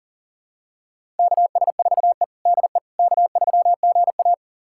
Morse code, heard as a keyed tone, about 40 wpm, 700 Hz. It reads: KS4E DE K3GA
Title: USA Callsign DE Callsign - 4 Characters - 40wpm